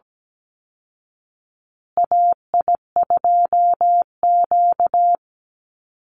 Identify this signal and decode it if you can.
Morse code, heard as a keyed tone, about 17 wpm, 700 Hz. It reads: AI2Q
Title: Call Signs Repeated 2x - 17wpm